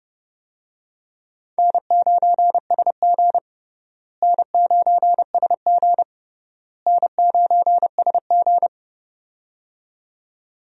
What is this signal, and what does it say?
Morse code, heard as a keyed tone, about 30 wpm, 700 Hz. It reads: N9SG N9SG N9SG
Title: USA Call Signs - 4 Characters - Repeated 3x - 30wpm